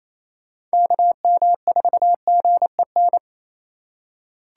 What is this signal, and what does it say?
Morse code, heard as a keyed tone, about 28 wpm, 700 Hz. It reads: KM4GEN
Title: USA Call Signs - 6 Characters - 4x Character Spacing - 28wpm